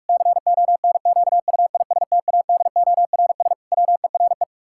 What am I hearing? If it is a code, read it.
Morse code, heard as a keyed tone, about 44 wpm, 700 Hz. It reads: CONQUISTADORS WERE
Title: Pangram Challenge v2 - 45wpm